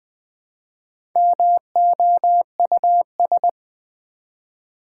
Morse code, heard as a keyed tone, about 20 wpm, 700 Hz. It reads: MOUS
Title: Build A Word - 1 Letter At A Time - 5-Letter Words - 20wpm